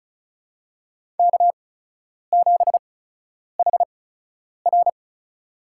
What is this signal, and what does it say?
Morse code, heard as a keyed tone, about 36 wpm, spaced as for 10 wpm, 700 Hz. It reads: K7HR
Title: USA Call Signs - 4 Characters - 8x Character Spacing - 35wpm